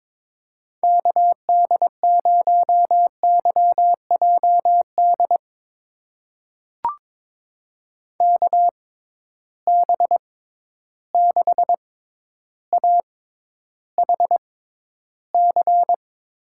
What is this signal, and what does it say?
Morse code, heard as a keyed tone, about 22 wpm, spaced as for 11 wpm, 700 Hz. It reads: KD0YJD KB6AHC
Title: USA Call Signs - 6 Characters - 6x Character Spacing - 22wpm